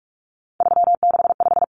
Morse code, heard as a keyed tone, about 44 wpm, 700 Hz. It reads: 365
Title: Common Cut Numbers - 3-Digit 45wpm